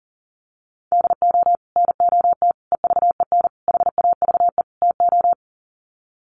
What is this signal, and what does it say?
Morse code, heard as a keyed tone, about 40 wpm, 700 Hz. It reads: DO NOT EVEN HAVE TO
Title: Sets of 5 Words Encore2 40wpm